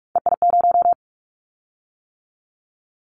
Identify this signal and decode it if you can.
Morse code, heard as a keyed tone, about 46 wpm, 700 Hz. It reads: EI0
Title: Callsign Prefixes for Top 50 DXCC - 45wpm